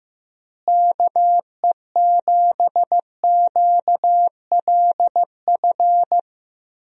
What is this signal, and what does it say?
Morse code, heard as a keyed tone, about 15 wpm, 700 Hz. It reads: KE7QLF
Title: USA Call Signs - 6 Characters - Repeated 2x - 15wpm